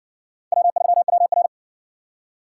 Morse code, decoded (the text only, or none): W3WR